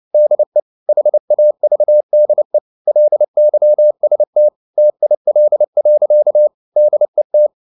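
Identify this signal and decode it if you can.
Morse code, heard as a keyed tone, about 29 wpm, 600 Hz. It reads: DE HAVDE LYST TIL. DET
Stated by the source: Fast (QRQ) Morse at 29 wpm, from CW med Gnister.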